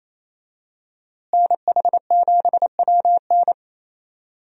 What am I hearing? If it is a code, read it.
Morse code, heard as a keyed tone, about 28 wpm, 700 Hz. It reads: NH7WN
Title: USA Call Signs - 5 Characters - Repeated 2x - 28wpm